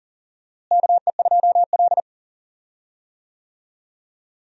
Morse code, heard as a keyed tone, about 40 wpm, 700 Hz. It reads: KE2L
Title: USA Call Signs - 4 Characters - Repeated 3x - 40wpm